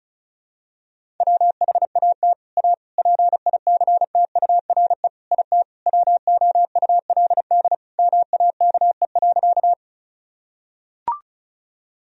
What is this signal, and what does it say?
Morse code, heard as a keyed tone, about 36 wpm, 700 Hz. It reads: WHAT A PICTURE IT WOULD MAKE.
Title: Sentences from Top 1000 Words - Review Part 1 - 35wpm